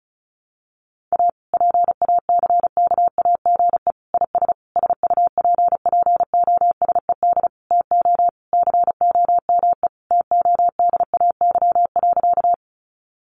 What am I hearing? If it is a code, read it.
Morse code, heard as a keyed tone, about 34 wpm, 700 Hz. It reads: A PACKAGE IS SUPPOSED TO COME TODAY.
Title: Easy Phrases 35wpm